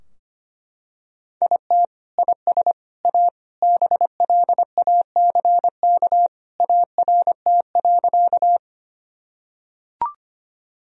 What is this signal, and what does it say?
Morse code, heard as a keyed tone, about 25 wpm, 700 Hz. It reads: IT IS A BLACK ART.
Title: Sentences from Top 1000 Words - Review Part 3 - 25wpm